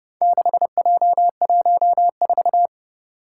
Morse code, heard as a keyed tone, about 30 wpm, 700 Hz. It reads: 6J14
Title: Four Challenging Letter-Number Combinations 30wpm